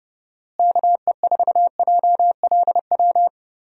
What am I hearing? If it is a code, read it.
Morse code, heard as a keyed tone, about 32 wpm, 700 Hz. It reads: KE4JLW
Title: USA Call Signs - 6 Characters - Repeated 2x - 30wpm